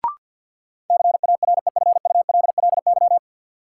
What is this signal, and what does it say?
Morse code, heard as a keyed tone, about 51 wpm, 700 Hz. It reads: CAREFULLY
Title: Sentences from Top 1300 Words - New Word Drill Alt - 50wpm